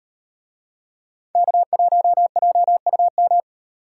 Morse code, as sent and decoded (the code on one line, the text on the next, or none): -.- .---- .--- ..- --
K1JUM